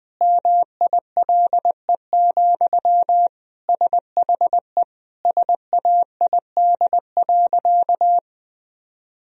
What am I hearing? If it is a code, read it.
Morse code, heard as a keyed tone, about 20 wpm, 700 Hz. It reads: MILE, SHE SAID.
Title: Sentences from Top 1000 Words - Review Part 1 - 20wpm